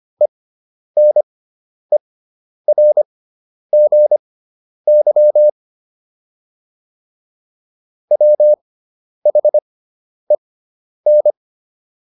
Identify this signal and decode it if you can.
Morse code, heard as a keyed tone, about 25 wpm, spaced as for 10 wpm, 600 Hz. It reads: ENERGY WHEN